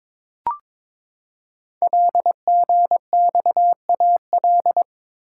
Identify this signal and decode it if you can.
Morse code, heard as a keyed tone, about 22 wpm, 700 Hz. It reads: LGXAL
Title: Code Groups - 5 Letters - 22wpm